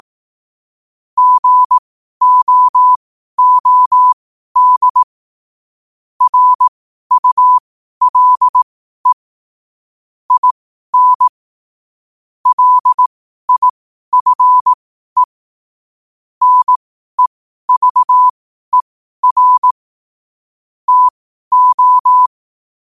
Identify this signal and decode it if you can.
Morse code, heard as a keyed tone, about 18 wpm, spaced as for 12 wpm, 1000 Hz. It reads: GOOD RULE IN LIFE NEVER TO
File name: qotdfiles / 12wpm